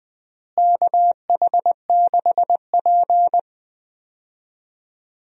Morse code, heard as a keyed tone, about 21 wpm, 690 Hz. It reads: KH6P
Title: DXCC-100 Call Signs - 4 Characters - Repeated 3x - 20wpm